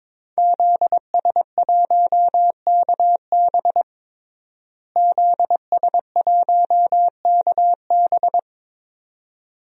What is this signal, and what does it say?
Morse code, heard as a keyed tone, about 22 wpm, 700 Hz. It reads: ZS1KB ZS1KB
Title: DXCC-100 Call Signs - 5 Characters - Repeated 2x - 22wpm